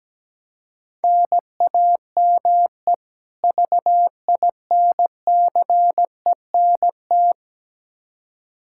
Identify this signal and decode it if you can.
Morse code, heard as a keyed tone, about 17 wpm, 700 Hz. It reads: NAME VINCENT
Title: Practice QSO - Exchange 1 - Beginning - 17wpm